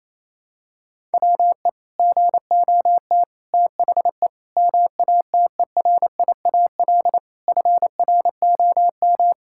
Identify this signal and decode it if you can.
Morse code, heard as a keyed tone, about 28 wpm, 700 Hz. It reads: WE GOT THE MATERIAL FROM
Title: Sentences from Top 1000 Words - Review Part 5 - 28wpm